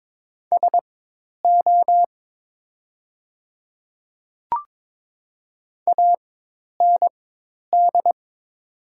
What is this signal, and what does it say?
Morse code, heard as a keyed tone, about 22 wpm, spaced as for 10 wpm, 700 Hz. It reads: SO AND
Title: 4x Character Spacing - Top 100 Words - 22wpm